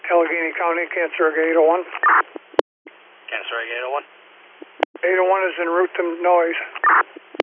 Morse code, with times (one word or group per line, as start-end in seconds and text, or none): none